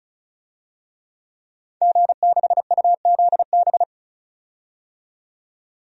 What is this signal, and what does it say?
Morse code, heard as a keyed tone, about 35 wpm, 700 Hz. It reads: G6UZB
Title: DXCC-100 Call Signs - 5 Characters - 35wpm